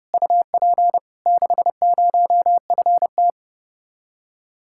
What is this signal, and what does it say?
Morse code, heard as a keyed tone, about 31 wpm, 700 Hz. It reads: UP 60FT